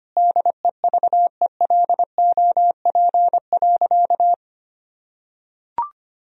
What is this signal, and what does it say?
Morse code, heard as a keyed tone, about 25 wpm, 700 Hz. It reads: DEVELOP.
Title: Sentences from Top 1000 Words - Review Part 2 - 25wpm